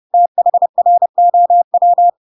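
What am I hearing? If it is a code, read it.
Morse code, heard as a keyed tone, about 30 wpm, 700 Hz. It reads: THROW
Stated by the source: Chapter 2 of The Adventures of Pinocchio read as a book in Morse code (CW) at 30 wpm, by ik1hge.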